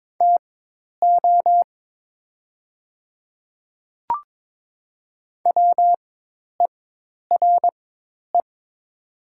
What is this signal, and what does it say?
Morse code, heard as a keyed tone, about 22 wpm, spaced as for 10 wpm, 700 Hz. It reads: TO WERE
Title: 4x Character Spacing - Top 200 Words - 22wpm